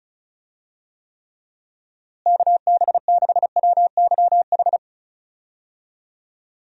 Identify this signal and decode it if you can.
Morse code, heard as a keyed tone, about 34 wpm, 700 Hz. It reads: KB6WYH
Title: USA Call Signs - 6 Characters - Repeated 3x - 35wpm